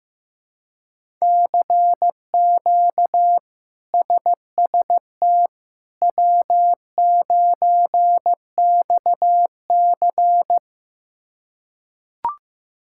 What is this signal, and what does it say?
Morse code, heard as a keyed tone, about 15 wpm, 700 Hz. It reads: CQ SST W9XC
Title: K1USN SST QSO Practice - 15wpm